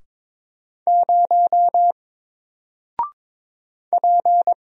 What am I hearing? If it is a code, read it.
Morse code, heard as a keyed tone, about 22 wpm, 700 Hz. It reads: 0 P
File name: Single-Letter-Number-22wpm